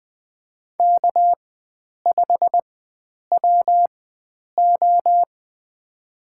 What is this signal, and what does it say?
Morse code, heard as a keyed tone, about 20 wpm, spaced as for 9 wpm, 700 Hz. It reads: K5WO
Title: USA Call Signs - 4 Characters - 4x Character Spacing - 20wpm